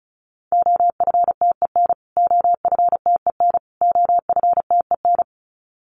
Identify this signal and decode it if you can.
Morse code, heard as a keyed tone, about 34 wpm, 700 Hz. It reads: OFTEN OFTEN OFTEN
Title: Top 500 Words Repeated 35wpm